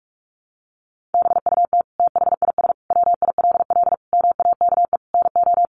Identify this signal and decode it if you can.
Morse code, heard as a keyed tone, about 44 wpm, 700 Hz. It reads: BUT THIS WILL MAKE NO
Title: Sentences from Top 800 Words 45wpm